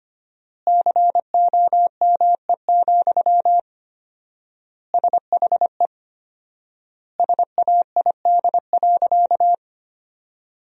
Morse code, heard as a keyed tone, about 25 wpm, 700 Hz. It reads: COME, SHE SAID.
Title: 4x Word-Spacing - Sentences from Top 100 Words - 25wpm